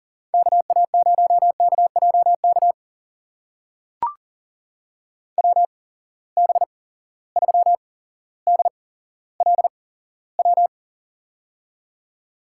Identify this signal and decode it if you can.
Morse code, heard as a keyed tone, about 38 wpm, spaced as for 17 wpm, 700 Hz. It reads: KA0KJK WB3DLW
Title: USA Call Signs - 6 Characters - 8x Character Spacing - 40wpm